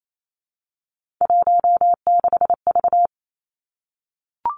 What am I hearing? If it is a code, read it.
Morse code, heard as a keyed tone, about 28 wpm, 700 Hz. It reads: 16V
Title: Three Challenging Letter-Number Combinations 28wpm